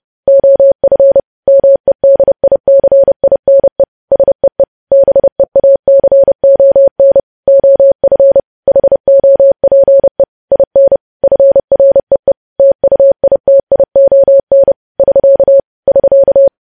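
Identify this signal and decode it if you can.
Morse code, heard as a keyed tone, about 30 wpm, 550 Hz. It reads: OF MEDICINE SEE BEACON OF HOPE IN FREE TUITION <SK> <SK>
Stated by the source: News headlines sent in Morse code at 30.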